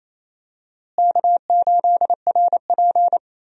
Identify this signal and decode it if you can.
Morse code, heard as a keyed tone, about 28 wpm, 700 Hz. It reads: K8RP
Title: USA Call Signs - 4 Characters - Repeated 3x - 28wpm